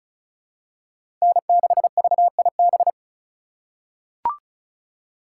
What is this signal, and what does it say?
Morse code, heard as a keyed tone, about 36 wpm, 700 Hz. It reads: N6VIB
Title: USA Call Signs - 5 Characters - 3x Character Spacing - 35wpm